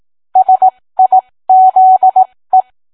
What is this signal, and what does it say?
Morse code, heard as a keyed tone, about 18 wpm, spaced as for 15 wpm, 750 Hz. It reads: SIZE